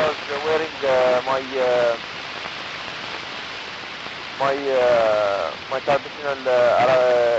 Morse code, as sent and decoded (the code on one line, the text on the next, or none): none